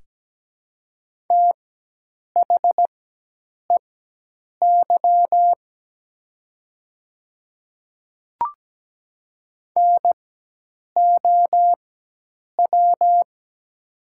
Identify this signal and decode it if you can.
Morse code, heard as a keyed tone, about 17 wpm, spaced as for 8 wpm, 700 Hz. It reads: THEY NOW
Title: 4x Character Spacing - Top 100 Words - 17wpm